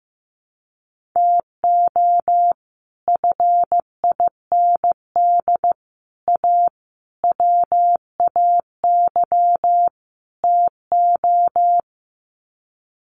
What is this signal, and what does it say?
Morse code, heard as a keyed tone, about 15 wpm, 700 Hz. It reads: TO FIND A WAY TO
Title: Sets of 5 Words 15wpm